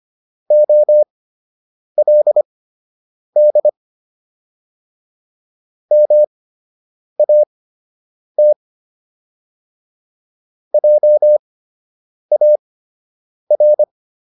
Morse code, heard as a keyed tone, about 25 wpm, spaced as for 8 wpm, 600 Hz. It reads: OLD MAT JAR